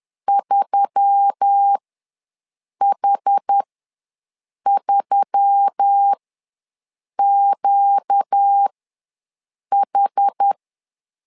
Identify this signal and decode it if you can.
Morse code, heard as a keyed tone, about 10 wpm, spaced as for 6 wpm, 800 Hz. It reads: 3H3QH
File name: LEZIONI / 4 Q1S3 ws 10wpm 2437